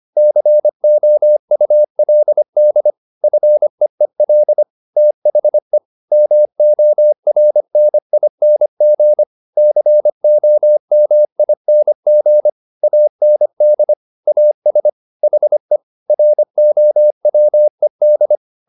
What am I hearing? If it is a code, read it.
Morse code, heard as a keyed tone, about 25 wpm, 600 Hz. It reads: COULD FEEL THE MORNING COMING AND AS HE ROWED
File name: the old man and the sea part 1 25wpm